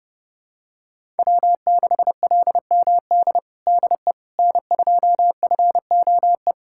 Morse code, heard as a keyed tone, about 30 wpm, 700 Hz. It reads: W6LMD DE N2FOE
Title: USA Callsign DE Callsign - 5 Characters - 30wpm